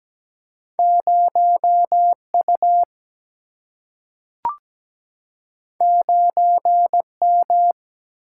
Morse code, heard as a keyed tone, about 17 wpm, 700 Hz. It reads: 0U 9M
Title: Single-Letter Single-Number 17wpm